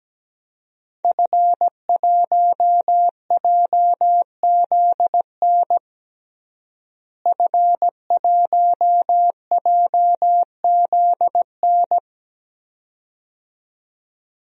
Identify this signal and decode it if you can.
Morse code, heard as a keyed tone, about 17 wpm, 700 Hz. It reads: F1JZN F1JZN
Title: DXCC-100 Call Signs - 5 Characters - Repeated 2x - 17wpm